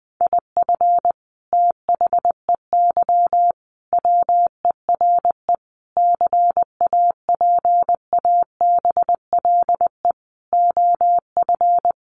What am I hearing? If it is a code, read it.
Morse code, heard as a keyed tone, about 20 wpm, 700 Hz. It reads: IF THEY WERE CAPABLE OF
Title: Sentences from Top 1200 Words 20wpm